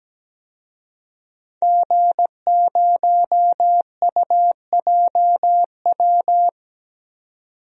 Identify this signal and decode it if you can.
Morse code, heard as a keyed tone, about 17 wpm, 700 Hz. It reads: G0UJW
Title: DXCC-100 Call Signs - 5 Characters - 17wpm